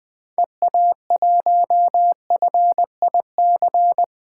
Morse code, heard as a keyed tone, about 20 wpm, 700 Hz. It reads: EA1FIC
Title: 2023 POTA Hunter Call Signs - 20wpm